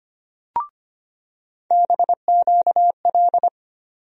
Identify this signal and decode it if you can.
Morse code, heard as a keyed tone, about 25 wpm, 700 Hz. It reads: BQL
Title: USA Call Sign Postfixes - 3 Letters 25wpm